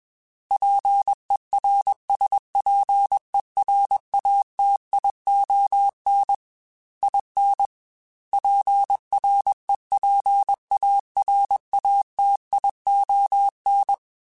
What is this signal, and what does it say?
Morse code, heard as a keyed tone, about 21 wpm, 800 Hz. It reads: PERSPERATION IN PREPARATION
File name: Quotes / Quotes-mp3s / Persperation @20